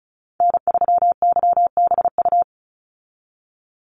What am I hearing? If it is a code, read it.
Morse code, heard as a keyed tone, about 34 wpm, 700 Hz. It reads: N3YBU
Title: USA Call Signs 35wpm